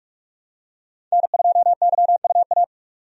Morse code, heard as a keyed tone, about 46 wpm, 700 Hz. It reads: N2YUA
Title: USA Call Signs - 5 Characters - 45wpm